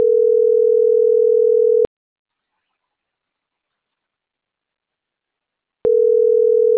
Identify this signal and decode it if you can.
Ringback tone (North America), 440 + 480 Hz.